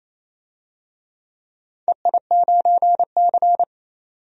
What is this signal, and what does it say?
Morse code, heard as a keyed tone, about 28 wpm, 700 Hz. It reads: EI9C
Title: DXCC-100 Call Signs - 4 Characters - 28wpm